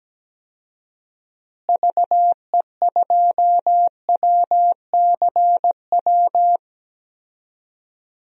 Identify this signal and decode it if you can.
Morse code, heard as a keyed tone, about 17 wpm, 700 Hz. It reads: VE2WCW
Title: DXCC-100 Call Signs - 6 Characters - Repeated 3x - 17wpm